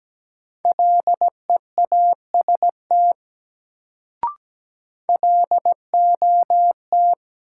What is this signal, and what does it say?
Morse code, heard as a keyed tone, about 17 wpm, 700 Hz. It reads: LEAST LOT
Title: Sets of 2 Words - New Word Drill - 17wpm